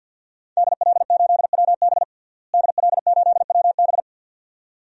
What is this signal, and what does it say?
Morse code, heard as a keyed tone, about 51 wpm, 700 Hz. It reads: DL8WB DL8WB
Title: DXCC-100 Call Signs - 5 Characters - Repeated 2x - 50wpm